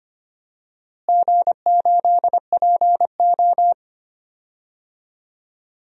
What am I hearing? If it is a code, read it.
Morse code, heard as a keyed tone, about 25 wpm, 700 Hz. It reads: G8PO
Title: DXCC-100 Call Signs - 4 Characters - Repeated 3x - 25wpm